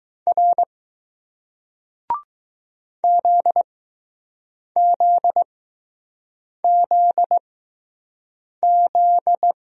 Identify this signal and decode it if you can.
Morse code, heard as a keyed tone, about 20 wpm, spaced as for 6 wpm, 700 Hz. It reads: R ZZZZ